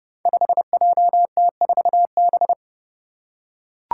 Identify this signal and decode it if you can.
Morse code, heard as a keyed tone, about 30 wpm, 700 Hz. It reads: 5JT4B
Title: Five Challenging Letter-Number Combinations 30wpm